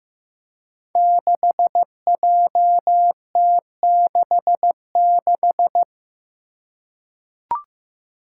Morse code, heard as a keyed tone, about 15 wpm, 700 Hz. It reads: 6JT66
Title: Five Challenging Letter-Number Combinations 15wpm